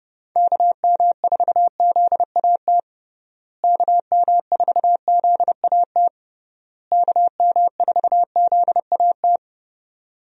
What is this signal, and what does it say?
Morse code, heard as a keyed tone, about 30 wpm, 700 Hz. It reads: KM4ZAT KM4ZAT KM4ZAT
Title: USA Call Signs - 6 Characters - Repeated 3x - 30wpm